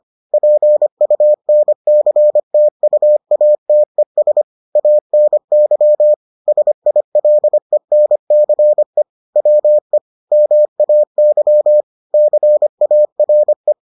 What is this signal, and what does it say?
Morse code, heard as a keyed tone, about 25 wpm, 600 Hz. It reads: PUNCTUATES ANY SILENCE WE MAY CARE